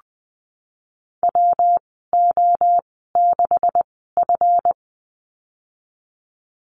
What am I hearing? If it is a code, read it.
Morse code, heard as a keyed tone, about 20 wpm, spaced as for 15 wpm, 700 Hz. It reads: WO6F
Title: 140 Lesson 23g - Letter f - Call Signs 20wpm